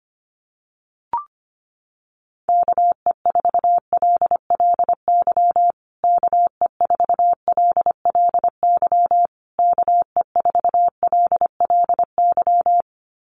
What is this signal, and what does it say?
Morse code, heard as a keyed tone, about 25 wpm, 700 Hz. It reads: KE4LLY KE4LLY KE4LLY
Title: USA Call Signs Repeated 3x 25wpm